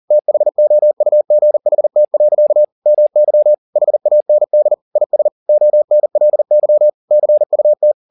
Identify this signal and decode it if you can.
Morse code, heard as a keyed tone, about 40 wpm, 600 Hz. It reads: THOUGHT. MY HAND IS ONLY CUT
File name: the old man and the sea 2 40wpm